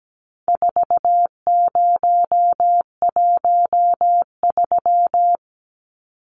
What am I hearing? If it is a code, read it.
Morse code, heard as a keyed tone, about 17 wpm, 700 Hz. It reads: 4013
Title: Numbers - 4-Digit 17wpm